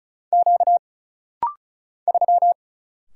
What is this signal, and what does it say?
Morse code, heard as a keyed tone, about 34 wpm, 700 Hz. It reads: Q 3
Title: Single Letter-Number - Rapid-Fire 35wpm